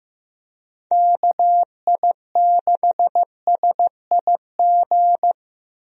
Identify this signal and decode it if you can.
Morse code, heard as a keyed tone, about 15 wpm, 700 Hz. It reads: KI6SIG
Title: USA Call Signs - 6 Characters - Repeated 3x - 15wpm